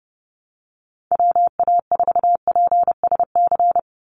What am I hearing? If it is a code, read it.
Morse code, heard as a keyed tone, about 30 wpm, 700 Hz. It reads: WA4PSC
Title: Call Signs - 30wpm